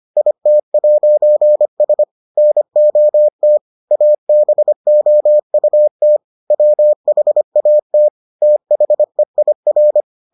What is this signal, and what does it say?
Morse code, heard as a keyed tone, about 25 wpm, 600 Hz. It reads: IT'S NOT ABOUT WHAT THEIR